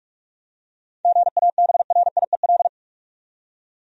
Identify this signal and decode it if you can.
Morse code, heard as a keyed tone, about 46 wpm, 700 Hz. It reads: GABRIEL
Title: USA Common Names - 45wpm